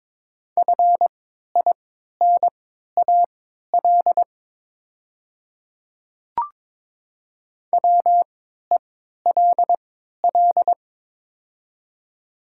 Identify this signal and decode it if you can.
Morse code, heard as a keyed tone, about 22 wpm, spaced as for 12 wpm, 700 Hz. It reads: FINAL WELL
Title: 3x Character Spacing - Top 500 Words - 22wpm